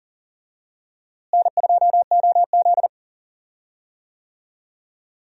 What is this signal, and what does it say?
Morse code, heard as a keyed tone, about 40 wpm, 700 Hz. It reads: N2OZ
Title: USA Call Signs - 4 Characters - Repeated 2x - 40wpm